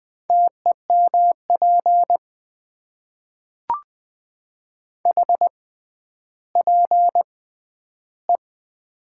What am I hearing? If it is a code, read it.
Morse code, heard as a keyed tone, about 20 wpm, spaced as for 11 wpm, 700 Hz. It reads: TEMP HPE
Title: Top 90 QSO Elements - 6x Character Spacing - 20wpm